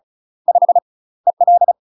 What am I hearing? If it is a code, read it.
Morse code, heard as a keyed tone, about 34 wpm, 700 Hz. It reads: 5 EL
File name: UK MASTER QSOs TRAINING / 35WPM / UK MASTER QSOs 0001